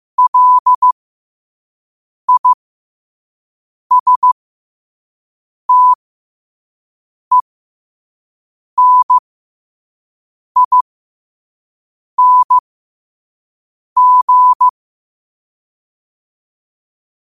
Morse code, heard as a keyed tone, about 15 wpm, spaced as for 5 wpm, 1000 Hz. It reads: LISTENING
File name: qotdfiles / 5wpm